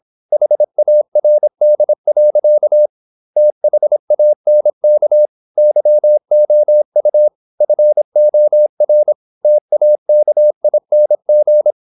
Morse code, heard as a keyed tone, about 26 wpm, 600 Hz. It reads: HARD. THANK YOU FOR TAKING
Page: Never 26wpm | CW med Gnister